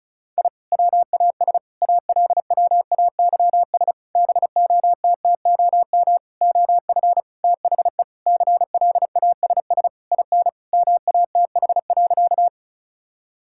Morse code, heard as a keyed tone, about 36 wpm, 700 Hz. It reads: I WAS ALWAYS BOTTOM OF THE CLASS IN MATH.
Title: Sentences from Top 1000 Words - Review Part 6 - 35wpm